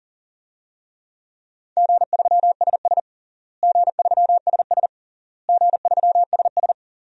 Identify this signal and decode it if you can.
Morse code, heard as a keyed tone, about 40 wpm, 700 Hz. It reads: G3SS G3SS G3SS
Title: DXCC-100 Call Signs - 4 Characters - Repeated 3x - 40wpm